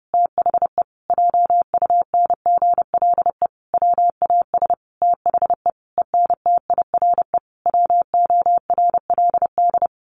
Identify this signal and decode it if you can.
Morse code, heard as a keyed tone, about 28 wpm, 700 Hz. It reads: THE JUNGLE WAS THE ENTIRE WORLD
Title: A Sound Of Thunder 30wpm